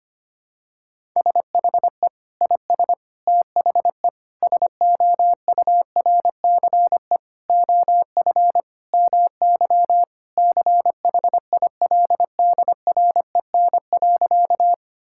Morse code, heard as a keyed tone, about 25 wpm, 700 Hz. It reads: SHE IS THE SOURCE OF MY CHILDREN.
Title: Sentences from Top 1000 Words - Review Part 5 - 25wpm